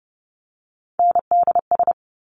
Morse code, heard as a keyed tone, about 30 wpm, 700 Hz. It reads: NDS
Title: Top Three Letter Combinations in Words 30wpm